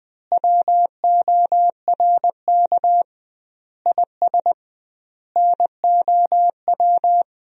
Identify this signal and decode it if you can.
Morse code, heard as a keyed tone, about 20 wpm, 700 Hz. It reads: WORK IS NOW
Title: 2x Word-Spacing - Sentences from Top 400 Words - 20wpm